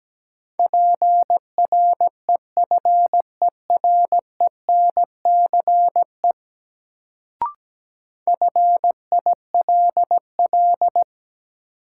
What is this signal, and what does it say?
Morse code, heard as a keyed tone, about 17 wpm, 700 Hz. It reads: PREFERENCE FILL
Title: Sentences from Top 1500 Words - New Word Drill Alt - 17wpm